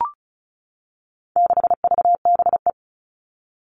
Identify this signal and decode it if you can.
Morse code, heard as a keyed tone, about 34 wpm, 700 Hz. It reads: 6VBE
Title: All Cut Numbers - 4-Digit 35wpm